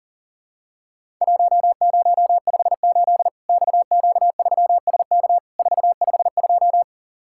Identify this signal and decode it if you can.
Morse code, heard as a keyed tone, about 40 wpm, 700 Hz. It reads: 1058 XQ3SK 452
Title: Worked All Europe DX Contest - QTC Exchange - 40wpm